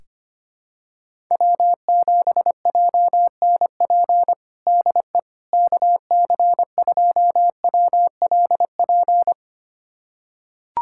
Morse code, heard as a keyed tone, about 25 wpm, 700 Hz. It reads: W7JNP DE KC2WLP
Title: USA Callsign DE Callsign 25wpm